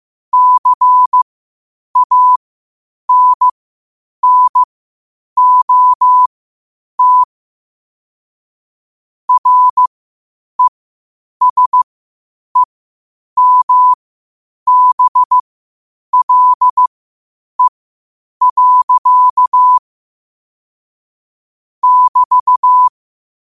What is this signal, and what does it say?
Morse code, heard as a keyed tone, about 15 wpm, spaced as for 8 wpm, 1000 Hz. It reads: CANNOT RESEMBLE. <BT>
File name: qotdfiles / 8wpm